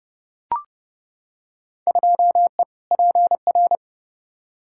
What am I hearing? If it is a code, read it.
Morse code, heard as a keyed tone, about 30 wpm, 700 Hz. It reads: 2E PR
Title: Field Day Exchanges Encore - 30wpm